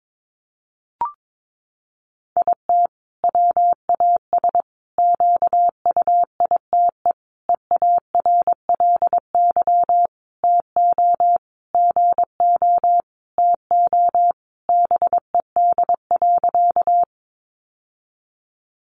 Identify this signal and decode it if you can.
Morse code, heard as a keyed tone, about 22 wpm, 700 Hz. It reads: IT WAS QUITE EARLY TO GO TO BED.
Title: Sentences from Top 700 Words 22wpm